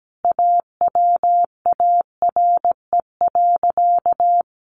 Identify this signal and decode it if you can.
Morse code, heard as a keyed tone, about 17 wpm, 700 Hz. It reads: AWARE.
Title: Sentences from Top 1300 Words 17wpm